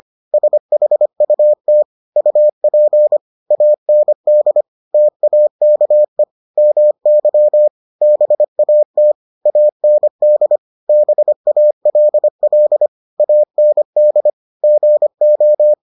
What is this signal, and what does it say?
Morse code, heard as a keyed tone, about 25 wpm, 600 Hz. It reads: SHUT UP AND TAKE MY BAT AND BALL AND GO